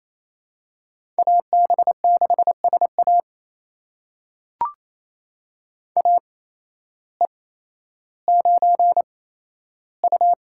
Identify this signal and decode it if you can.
Morse code, heard as a keyed tone, about 28 wpm, spaced as for 13 wpm, 700 Hz. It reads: AB6SA AE9U
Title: USA Call Signs - 8x Character Spacing - 28wpm